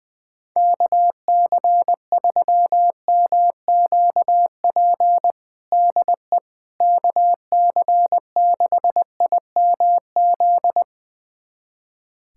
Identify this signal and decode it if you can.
Morse code, heard as a keyed tone, about 20 wpm, 700 Hz. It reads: KC3MQP DE KC6IMZ
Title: USA Callsign DE Callsign 20wpm